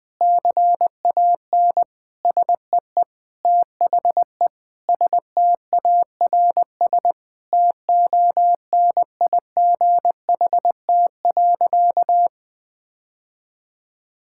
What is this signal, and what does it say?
Morse code, heard as a keyed tone, about 21 wpm, 700 Hz. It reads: CAN SEE THE STARS TONIGHT.
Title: Sentences with 3-Word Phrases - Spoken Beginning - 20wpm